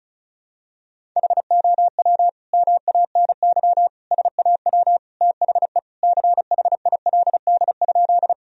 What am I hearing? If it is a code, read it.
Morse code, heard as a keyed tone, about 36 wpm, 700 Hz. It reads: HOW MANY SAW THE CHILD?
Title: Sentences from Top 1000 Words - Review Part 1 - 35wpm